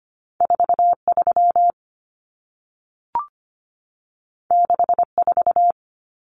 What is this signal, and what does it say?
Morse code, heard as a keyed tone, about 25 wpm, 700 Hz. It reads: 43 64
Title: Numbers - 2-Digit 25wpm